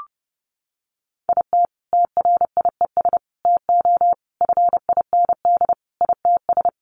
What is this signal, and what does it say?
Morse code, heard as a keyed tone, about 30 wpm, 700 Hz. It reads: IT TRIES TO FIND ITS
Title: Additional Easy Phrases 30wpm